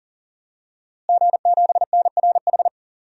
Morse code, heard as a keyed tone, about 37 wpm, 700 Hz. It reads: G7NRH
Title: DXCC-100 Call Signs - 5 Characters - Repeated 3x - 40wpm